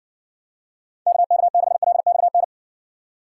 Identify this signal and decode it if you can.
Morse code, heard as a keyed tone, about 60 wpm, 700 Hz. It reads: KK6LXN